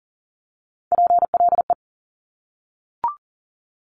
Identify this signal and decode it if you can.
Morse code, heard as a keyed tone, about 42 wpm, 700 Hz. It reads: PLE
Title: Top Three Letter Combinations in Words 40wpm